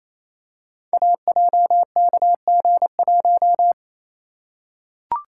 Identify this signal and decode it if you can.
Morse code, heard as a keyed tone, about 28 wpm, 700 Hz. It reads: AJKG1
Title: Code Groups - 5 Letter:Numbers - 28wpm